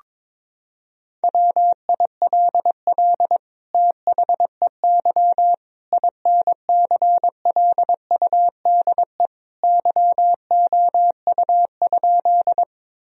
Morse code, heard as a keyed tone, about 21 wpm, 700 Hz. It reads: WILL THEY INCLUDE YOU?
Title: Sentences from Top 1000 Words - Review Part 3 - 22wpm